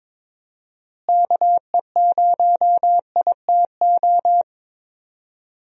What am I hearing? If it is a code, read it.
Morse code, heard as a keyed tone, about 21 wpm, 700 Hz. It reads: KE0ITO
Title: USA Call Signs - 6 Characters - Repeated 2x - 22wpm